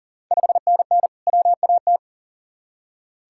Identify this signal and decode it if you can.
Morse code, heard as a keyed tone, about 40 wpm, 700 Hz. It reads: 5NN WAT
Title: Ontario QSO Party Exchange 40wpm